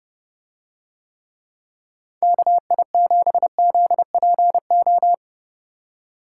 Morse code, heard as a keyed tone, about 29 wpm, 700 Hz. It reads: KI7ZPO